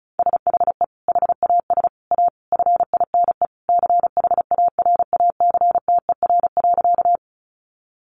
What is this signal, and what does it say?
Morse code, heard as a keyed tone, about 34 wpm, 700 Hz. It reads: SHE HAS A FINE CHARACTER.
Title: Sentences from Top 1200 Words 35wpm